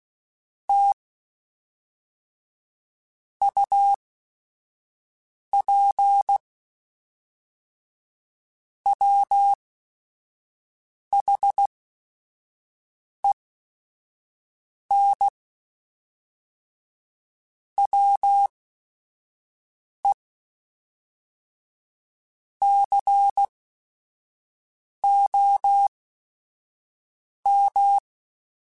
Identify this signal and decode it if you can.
Morse code, heard as a keyed tone, about 16 wpm, spaced as for 5 wpm, 800 Hz. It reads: T UP WHEN WE COM
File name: QSOs / QSOs-mp3s / WD0BGP 2@5